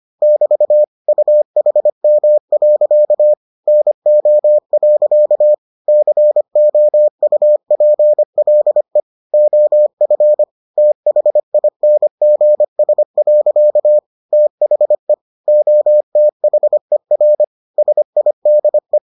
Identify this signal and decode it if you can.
Morse code, heard as a keyed tone, about 25 wpm, 600 Hz. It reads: <BT> UHM. NO. COUPLE OF THINGS. THE OTHER SIDE